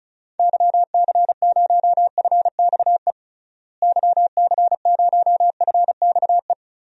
Morse code, heard as a keyed tone, about 36 wpm, 700 Hz. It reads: YC0FXE YC0FXE
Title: DXCC-100 Call Signs - 6 Characters - Repeated 2x - 35wpm